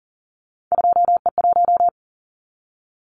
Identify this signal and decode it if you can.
Morse code, heard as a keyed tone, about 38 wpm, 700 Hz. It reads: 2E1
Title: Callsign Prefixes for Top 25 DXCC - 40wpm